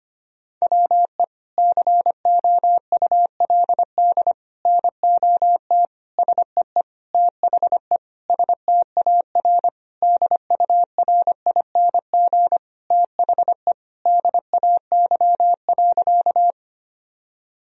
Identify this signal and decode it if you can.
Morse code, heard as a keyed tone, about 25 wpm, 700 Hz. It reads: WE COULD NOT SEE THE STAR DURING THE DAY.
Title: Sentences from Top 1000 Words - Review Part 2 - 25wpm